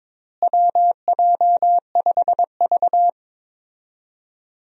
Morse code, heard as a keyed tone, about 22 wpm, 700 Hz. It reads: WJ5V
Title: USA Call Signs - 4 Characters - Repeated 3x - 22wpm